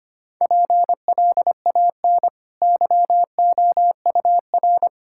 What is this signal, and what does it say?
Morse code, heard as a keyed tone, about 25 wpm, 700 Hz. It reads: PLAN YOUR
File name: Easy Sentences with Hints - 25wpm